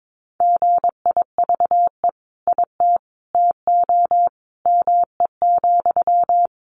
Morse code, heard as a keyed tone, about 22 wpm, 700 Hz. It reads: GIVE IT TO ME,
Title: Sentences from Top 200 Words 22wpm